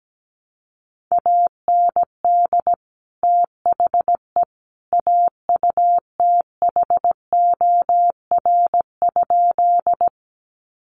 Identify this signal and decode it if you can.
Morse code, heard as a keyed tone, about 17 wpm, 700 Hz. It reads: AND THE AUTHOR?
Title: Sentences from Top 1200 Words 17wpm